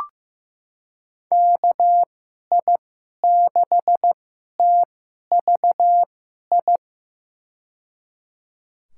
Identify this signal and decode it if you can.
Morse code, heard as a keyed tone, about 15 wpm, spaced as for 11 wpm, 700 Hz. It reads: KI6TVI
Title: USA Call Signs - 6 Characters - 2x Character Spacing - 15wpm